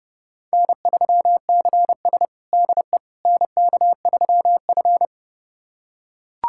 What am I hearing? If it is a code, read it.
Morse code, heard as a keyed tone, about 30 wpm, 700 Hz. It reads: N3CS DE NK3F
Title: USA Callsign DE Callsign - 4 Characters - 30wpm